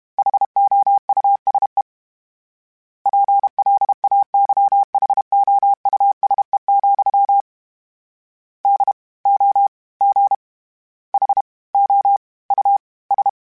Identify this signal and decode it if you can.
Morse code, heard as a keyed tone, about 32 wpm, 800 Hz. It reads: HOUSE PLAYHOUSE, D O G H O U S
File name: General / General-mp3s / Houses @30